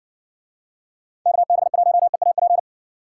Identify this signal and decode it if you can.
Morse code, heard as a keyed tone, about 59 wpm, 700 Hz. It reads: KB1EAP